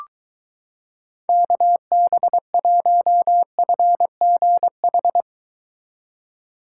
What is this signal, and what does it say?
Morse code, heard as a keyed tone, about 23 wpm, 700 Hz. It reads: KB1FGH